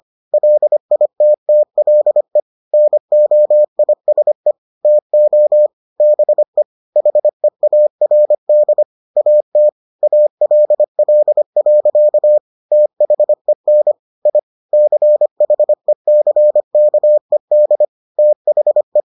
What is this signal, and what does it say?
Morse code, heard as a keyed tone, about 25 wpm, 600 Hz. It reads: LITTLE NOISE TO BE HEARD AT ALL. THEN I CHECKED THE